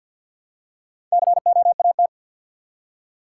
Morse code, heard as a keyed tone, about 50 wpm, 700 Hz. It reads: COAT
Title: Four Letter Words - 50wpm